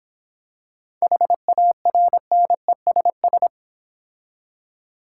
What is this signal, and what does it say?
Morse code, heard as a keyed tone, about 26 wpm, 700 Hz. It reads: HARNESS